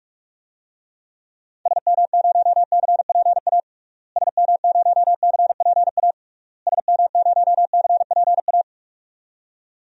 Morse code, heard as a keyed tone, about 43 wpm, 700 Hz. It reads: SM0CPA SM0CPA SM0CPA
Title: DXCC-100 Call Signs - 6 Characters - Repeated 3x - 45wpm